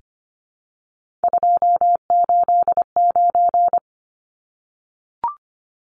Morse code, heard as a keyed tone, about 25 wpm, 690 Hz. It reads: 289
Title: Numbers - 3-Digit 25wpm